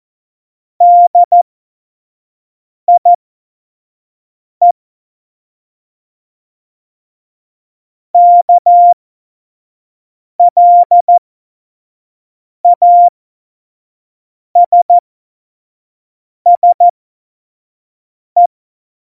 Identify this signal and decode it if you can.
Morse code, heard as a keyed tone, about 14 wpm, spaced as for 5 wpm, 700 Hz. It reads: DIE KLASSE